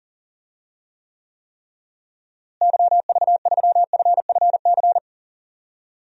Morse code, heard as a keyed tone, about 40 wpm, 700 Hz. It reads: YV3FFC
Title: DXCC-100 Call Signs - 6 Characters - Repeated 3x - 40wpm